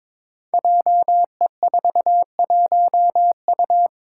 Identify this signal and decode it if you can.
Morse code, heard as a keyed tone, about 22 wpm, 700 Hz. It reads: JE41U
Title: Five Challenging Letter-Number Combinations 22wpm